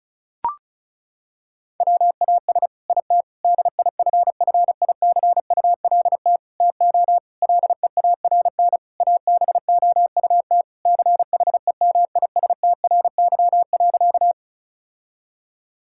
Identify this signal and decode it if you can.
Morse code, heard as a keyed tone, about 34 wpm, 700 Hz. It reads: WAS IT DIFFICULT TO LEARN ABOUT CHEMISTRY.
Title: Sentences from Top 1000 Words - Review Part 6 - 35wpm